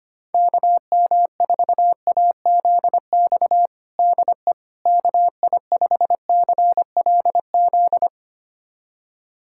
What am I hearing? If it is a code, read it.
Morse code, heard as a keyed tone, about 25 wpm, 700 Hz. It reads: KM4AZX DE KI5CLZ
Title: USA Callsign DE Callsign - 6 Characters - 25wpm